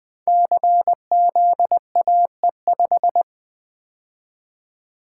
Morse code, heard as a keyed tone, about 20 wpm, 700 Hz. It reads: CZAE5
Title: Code Groups - 5 Letter:Numbers - 20wpm